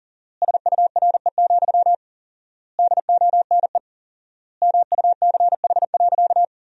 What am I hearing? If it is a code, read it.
Morse code, heard as a keyed tone, about 37 wpm, 700 Hz. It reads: SURE, DONE MUCH.
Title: 4x Word-Spacing - Sentences from Top 500 Words - 40wpm